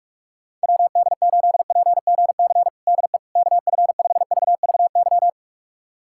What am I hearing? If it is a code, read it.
Morse code, heard as a keyed tone, about 46 wpm, 700 Hz. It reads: WD8PGC DE KF5VVY
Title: USA Callsign DE Callsign - 6 Characters - 45wpm